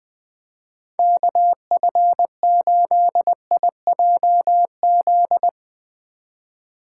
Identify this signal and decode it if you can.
Morse code, heard as a keyed tone, about 20 wpm, 700 Hz. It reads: KF8IJZ
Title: Speed Racing - USA 2x3 Call Signs 17wpm